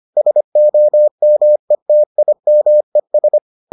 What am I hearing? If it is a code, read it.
Morse code, heard as a keyed tone, about 25 wpm, 600 Hz. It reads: SOMETIMES